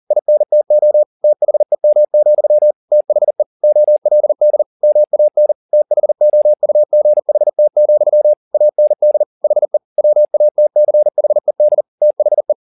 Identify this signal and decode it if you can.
Morse code, heard as a keyed tone, about 40 wpm, 600 Hz. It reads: INTO THEM, THE OLD MAN THOUGHT, AND HE WATCHED THE